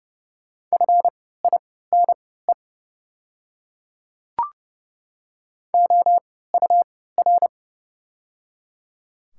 Morse code, heard as a keyed tone, about 31 wpm, spaced as for 17 wpm, 690 Hz. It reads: FINE OUR
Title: 3x Character Spacing - Top 500 Words - 30wpm